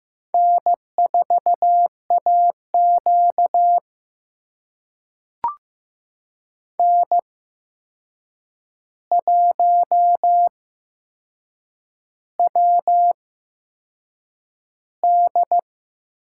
Morse code, heard as a keyed tone, about 15 wpm, spaced as for 6 wpm, 700 Hz. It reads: N4AQ N1WD
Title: USA Call Signs - 4 Characters - 8x Character Spacing - 15wpm